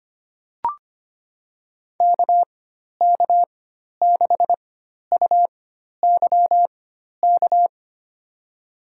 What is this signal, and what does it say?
Morse code, heard as a keyed tone, about 25 wpm, spaced as for 12 wpm, 700 Hz. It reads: KK6UYK
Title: USA Call Signs - 6 Characters - 4x Character Spacing - 25wpm